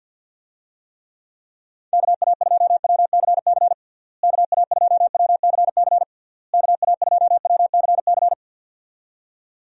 Morse code, heard as a keyed tone, about 51 wpm, 700 Hz. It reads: KA2WCC KA2WCC KA2WCC
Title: USA Call Signs - 6 Characters - Repeated 3x - 50wpm